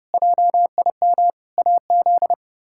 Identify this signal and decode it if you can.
Morse code, heard as a keyed tone, about 32 wpm, 700 Hz. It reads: JIM AZ
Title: K1USN SST QSO Practice - 30wpm